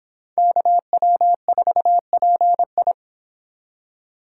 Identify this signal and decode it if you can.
Morse code, heard as a keyed tone, about 25 wpm, 700 Hz. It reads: KW4PI